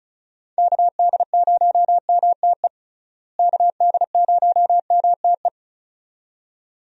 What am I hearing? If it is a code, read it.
Morse code, heard as a keyed tone, about 34 wpm, 700 Hz. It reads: KD0MTE KD0MTE
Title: USA Call Signs - 6 Characters - Repeated 2x - 35wpm